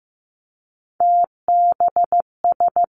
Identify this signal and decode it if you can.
Morse code, heard as a keyed tone, about 15 wpm, 700 Hz. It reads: TBS
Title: Three Challenging Letter-Number Combinations 15wpm